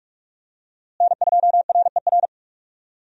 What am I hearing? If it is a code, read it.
Morse code, heard as a keyed tone, about 44 wpm, 700 Hz. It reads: N2RER
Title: USA Call Signs - 5 Characters - Repeated 2x - 45wpm